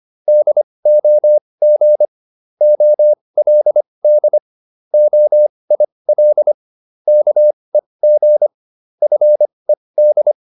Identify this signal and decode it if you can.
Morse code, heard as a keyed tone, about 25 wpm, spaced as for 20 wpm, 600 Hz. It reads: DOG OLD OIL KEG FED